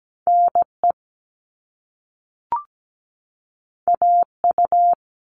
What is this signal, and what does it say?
Morse code, heard as a keyed tone, about 17 wpm, 700 Hz. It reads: NE AU
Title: All Cut Numbers - 2-Digit 17wpm